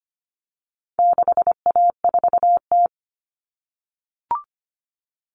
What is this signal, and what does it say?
Morse code, heard as a keyed tone, about 25 wpm, 700 Hz. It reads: 6A4T
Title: Common Cut Numbers - 4-Digit 25wpm